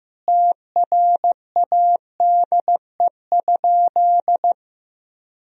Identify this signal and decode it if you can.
Morse code, heard as a keyed tone, about 15 wpm, 700 Hz. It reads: TRADE?
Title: Sentences from Top 1000 Words - Review Part 4 - 15wpm